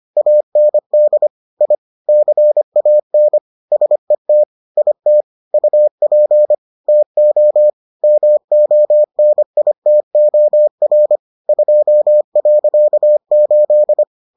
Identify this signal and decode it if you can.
Morse code, heard as a keyed tone, about 25 wpm, 600 Hz. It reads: AND I CAN SET IT UP TO MONITOR 2.8